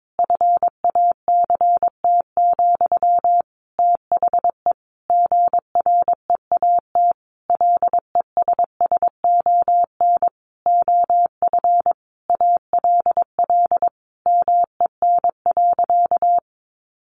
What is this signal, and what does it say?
Morse code, heard as a keyed tone, about 22 wpm, 700 Hz. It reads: FACT, THE GREAT LESSON OF ALL MEN.
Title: Sentences from Top 1200 Words 22wpm